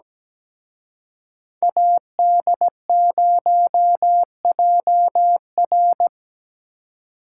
Morse code, heard as a keyed tone, about 20 wpm, 700 Hz. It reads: AD0JR
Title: Speed Racing - USA 2x2 Call Signs 17wpm